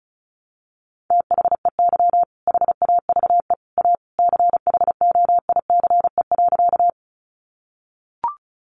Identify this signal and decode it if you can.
Morse code, heard as a keyed tone, about 34 wpm, 700 Hz. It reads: THEY HAVE A CHOICE.
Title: Sentences from Top 900 Words 35wpm